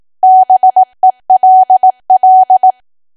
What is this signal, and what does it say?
Morse code, heard as a keyed tone, about 18 wpm, 750 Hz. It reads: BELL
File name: Archive / 18 WPM / 18WPM